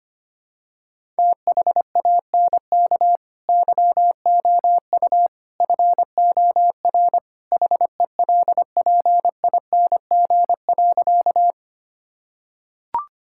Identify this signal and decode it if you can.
Morse code, heard as a keyed tone, about 25 wpm, 700 Hz. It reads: THANK YOU FOR HELPING.
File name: practice / 25wpm / Sentences with 3-Word Phrases - 25wpm